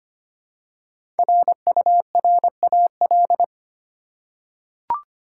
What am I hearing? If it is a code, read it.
Morse code, heard as a keyed tone, about 25 wpm, 700 Hz. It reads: RURAL
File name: 5 Letter Words - First 3 Letters Given - 25wpm